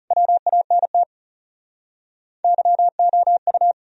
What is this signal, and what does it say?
Morse code, heard as a keyed tone, about 41 wpm, 700 Hz. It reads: WANT YOU